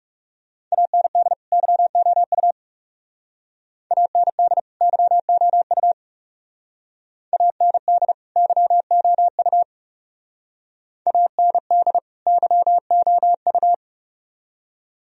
Morse code, heard as a keyed tone, about 36 wpm, 700 Hz. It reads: AND YOU AND YOU AND YOU AND YOU
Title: Speed-Racing - Sets of 2 Words 30wpm